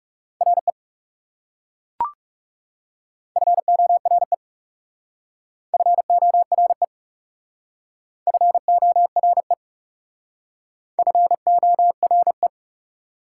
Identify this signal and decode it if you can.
Morse code, heard as a keyed tone, about 37 wpm, 700 Hz. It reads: RE FORE FORE FORE FORE
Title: Speed-Racing Prefixes and Suffixes 30wpm